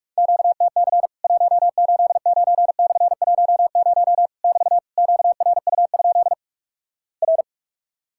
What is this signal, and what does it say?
Morse code, heard as a keyed tone, about 46 wpm, 700 Hz. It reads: QTC 189/10 <BT> QRU? R
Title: Worked All Europe DX Contest - QTC Exchange - 45wpm